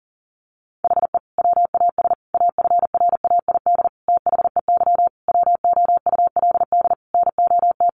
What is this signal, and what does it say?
Morse code, heard as a keyed tone, about 40 wpm, 700 Hz. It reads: HE WAS AFRAID THEY WOULD NOT
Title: Sentences from Top 1500 Words 40wpm